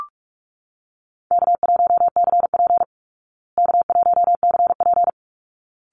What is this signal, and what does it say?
Morse code, heard as a keyed tone, about 44 wpm, 700 Hz. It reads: K1CP K1CP
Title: Easy Call Signs Repeated 2x - 45wpm